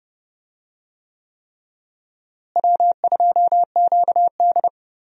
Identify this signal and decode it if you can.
Morse code, heard as a keyed tone, about 30 wpm, 700 Hz. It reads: W2QD
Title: USA Call Signs - 4 Characters - Repeated 3x - 30wpm